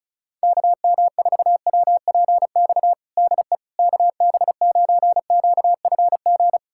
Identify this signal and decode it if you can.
Morse code, heard as a keyed tone, about 34 wpm, 700 Hz. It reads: KM4WPX DE KB9QFG
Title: USA Callsign DE Callsign - 6 Characters - 35wpm